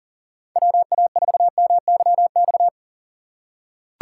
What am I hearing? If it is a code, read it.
Morse code, heard as a keyed tone, about 40 wpm, 700 Hz. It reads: WA4MYX
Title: USA Call Signs - 6 Characters - 4x Character Spacing - 40wpm